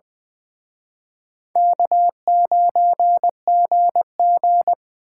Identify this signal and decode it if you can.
Morse code, heard as a keyed tone, about 23 wpm, 690 Hz. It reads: K9GG
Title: Speed Racing - USA 1x2 Call Signs 20wpm